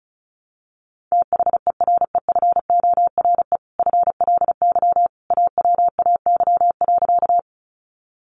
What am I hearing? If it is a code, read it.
Morse code, heard as a keyed tone, about 34 wpm, 700 Hz. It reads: THEREFORE FLY AWAY.
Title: Sentences from Top 800 Words 35wpm